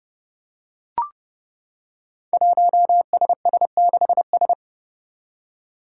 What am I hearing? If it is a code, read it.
Morse code, heard as a keyed tone, about 30 wpm, 700 Hz. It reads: 1SS6S
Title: Five Challenging Letter-Number Combinations 30wpm